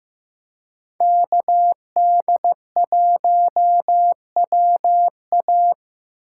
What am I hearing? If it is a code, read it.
Morse code, heard as a keyed tone, about 15 wpm, 700 Hz. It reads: KD1WA
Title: USA Call Signs - 3x Character Spacing - 15wpm